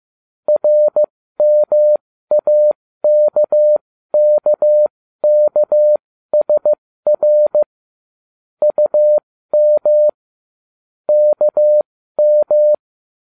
Morse code, heard as a keyed tone, about 15 wpm, spaced as for 13 wpm, 600 Hz. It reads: RMAKKKSR UM KM
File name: ubacw05 2